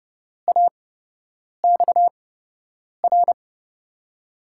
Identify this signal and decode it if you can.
Morse code, heard as a keyed tone, about 31 wpm, spaced as for 8 wpm, 700 Hz. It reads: AXR
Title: USA Call Signs - 5 Characters - 8x Character Spacing - 30wpm